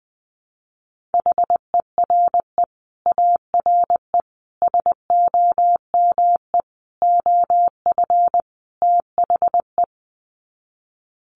Morse code, heard as a keyed tone, about 20 wpm, 700 Hz. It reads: HERE ARE SOME OF THE
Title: Sets of 5 Words Encore 20wpm